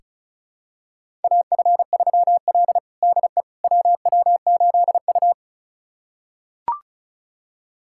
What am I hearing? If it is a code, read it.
Morse code, heard as a keyed tone, about 36 wpm, 700 Hz. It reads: AF3L DE WW8U
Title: USA Callsign DE Callsign - 4 Characters - 35wpm